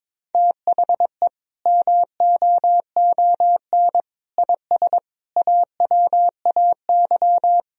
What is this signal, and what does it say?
Morse code, heard as a keyed tone, about 22 wpm, 700 Hz. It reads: THE MOON IS AWAY
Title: Sentences from Top 1000 Words - Review Part 3 - 22wpm